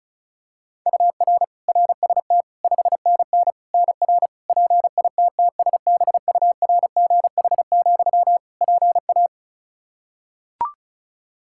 Morse code, heard as a keyed tone, about 34 wpm, 700 Hz. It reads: UR RST 5NN NR PITTSBURGH, PA
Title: Practice QSO - Exchange 1 - Beginning - 35wpm